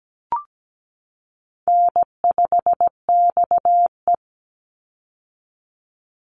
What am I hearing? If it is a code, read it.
Morse code, heard as a keyed tone, about 17 wpm, 700 Hz. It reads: N5XE
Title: Easy Call Signs - 17wpm